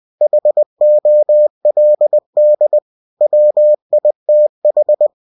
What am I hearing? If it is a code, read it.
Morse code, heard as a keyed tone, about 20 wpm, 600 Hz. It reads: HOLD WITH
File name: cw-books / winnie-the-pooh / winnie-20wpm-page-0085